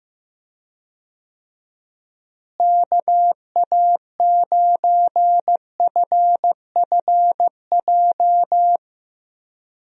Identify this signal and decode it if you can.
Morse code, heard as a keyed tone, about 15 wpm, 700 Hz. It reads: KA9FFJ
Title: USA Call Signs - 6 Characters - Repeated 3x - 15wpm